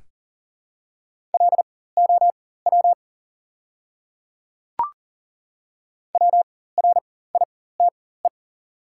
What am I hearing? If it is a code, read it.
Morse code, heard as a keyed tone, about 40 wpm, spaced as for 19 wpm, 700 Hz. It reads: LOW WRITE
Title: 4x Character Spacing - Top 200 Words - 40wpm